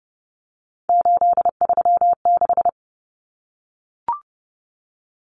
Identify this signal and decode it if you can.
Morse code, heard as a keyed tone, about 30 wpm, 700 Hz. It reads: 836
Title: Numbers - 3-Digit 30wpm